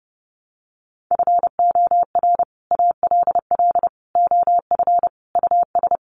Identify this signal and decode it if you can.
Morse code, heard as a keyed tone, about 30 wpm, 700 Hz. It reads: FOR ALL OF US
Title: Sets of 5 Words 30wpm